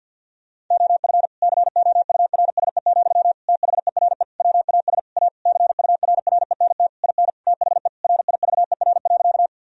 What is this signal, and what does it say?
Morse code, heard as a keyed tone, about 50 wpm, 700 Hz. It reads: OF COURSE, THERE WAS A CURRENT IN THE RIVER.
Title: Sentences from Top 1000 Words - Review Part 4 - 50wpm